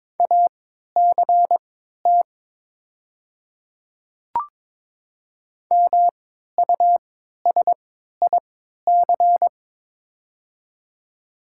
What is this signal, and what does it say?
Morse code, heard as a keyed tone, about 22 wpm, spaced as for 13 wpm, 700 Hz. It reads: ACT MUSIC
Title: 3x Character Spacing - Top 300 Words - 22wpm